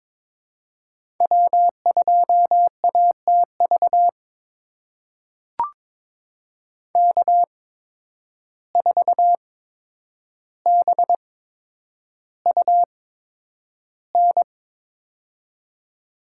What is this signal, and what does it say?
Morse code, heard as a keyed tone, about 22 wpm, spaced as for 9 wpm, 700 Hz. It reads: W2ATV K4BUN
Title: USA Call Signs - 5 Characters - 8x Character Spacing - 22wpm